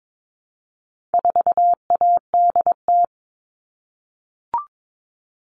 Morse code, heard as a keyed tone, about 22 wpm, 700 Hz. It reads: 4ADT
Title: All Cut Numbers - 4-Digit 22wpm